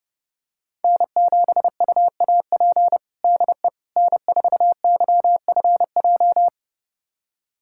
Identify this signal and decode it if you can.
Morse code, heard as a keyed tone, about 32 wpm, 700 Hz. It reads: N7UAP DE N4YFJ
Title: USA Callsign DE Callsign 30wpm